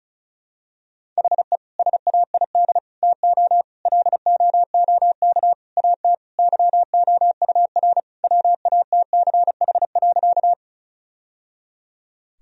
Morse code, heard as a keyed tone, about 36 wpm, 700 Hz. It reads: HE SAID TO LOOK AT YOUR WATCH.
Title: Sentences from Top 1000 Words - Review Part 1 - 35wpm